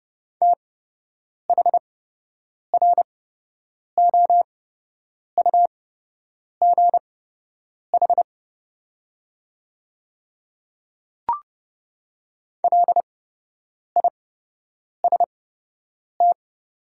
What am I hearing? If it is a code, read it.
Morse code, heard as a keyed tone, about 30 wpm, spaced as for 8 wpm, 690 Hz. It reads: THROUGH LIST
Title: 8x Character Spacing - Top 500 Words - 30wpm